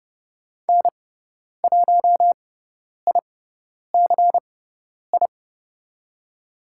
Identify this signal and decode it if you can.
Morse code, heard as a keyed tone, about 30 wpm, spaced as for 10 wpm, 700 Hz. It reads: N1ICI
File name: Lesson 18g - Letter m - Call Signs 30wpm